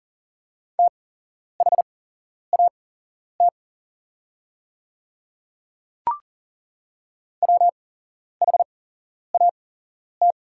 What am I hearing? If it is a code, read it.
Morse code, heard as a keyed tone, about 40 wpm, spaced as for 11 wpm, 700 Hz. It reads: THAT WHAT
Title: 8x Character Spacing - Top 200 Words - 40wpm